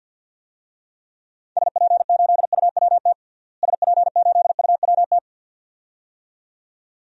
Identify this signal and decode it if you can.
Morse code, heard as a keyed tone, about 47 wpm, 700 Hz. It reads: SP8UWT SP8UWT
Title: DXCC-100 Call Signs - 6 Characters - Repeated 2x - 50wpm